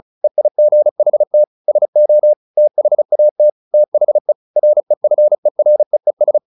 Morse code, heard as a keyed tone, about 34 wpm, 600 Hz. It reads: EIGHT SO THAT THE REFEREES
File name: the old man and the sea 2 35wpm